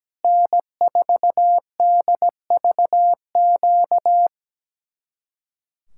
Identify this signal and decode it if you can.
Morse code, heard as a keyed tone, about 17 wpm, 700 Hz. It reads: N4DVQ
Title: Speed-Racing Common QSO Phrases - 17wpm